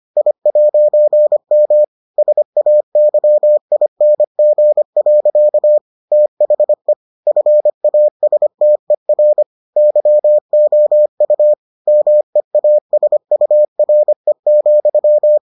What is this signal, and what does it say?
Morse code, heard as a keyed tone, about 25 wpm, 600 Hz. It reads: I'M SAYING. THE FASTER YOU MEASURE,